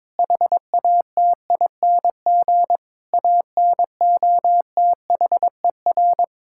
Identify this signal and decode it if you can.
Morse code, heard as a keyed tone, about 22 wpm, 700 Hz. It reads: HATING ANOTHER
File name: Sets of 9 Words - 22wpm